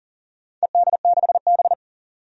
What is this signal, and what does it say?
Morse code, heard as a keyed tone, about 40 wpm, 700 Hz. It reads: ED6B
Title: DXCC-100 Call Signs - 4 Characters - 40wpm